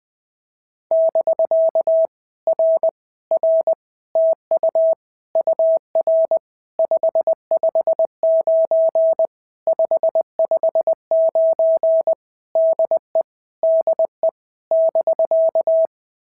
Morse code, heard as a keyed tone, about 20 wpm, 690 Hz. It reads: <BK> R R TU UR 559 559 DE DE <BK>
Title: POTA QSO Practice - 20wpm